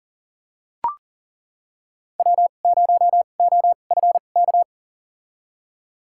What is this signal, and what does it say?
Morse code, heard as a keyed tone, about 40 wpm, spaced as for 29 wpm, 700 Hz. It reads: W0OFK
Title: USA Call Signs - 5 Characters - 2x Character Spacing - 40wpm